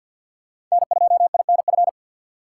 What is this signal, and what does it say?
Morse code, heard as a keyed tone, about 53 wpm, 700 Hz. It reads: N2INF
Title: USA Call Signs - 2x Character Spacing - 50wpm